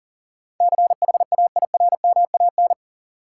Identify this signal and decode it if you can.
Morse code, heard as a keyed tone, about 40 wpm, 700 Hz. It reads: CHAIRMAN
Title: Sets of 3 Words - New Word Drill - 40wpm